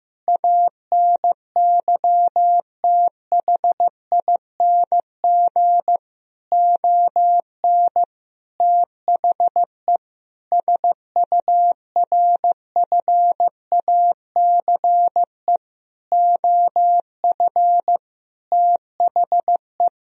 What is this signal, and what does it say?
Morse code, heard as a keyed tone, about 15 wpm, 700 Hz. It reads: ANYTHING ON THE SURFACE OF THE
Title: Sentences from Top 1000 Words - Review Part 3 - 15wpm